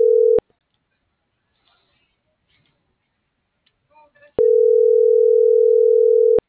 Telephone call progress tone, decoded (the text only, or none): ringback tone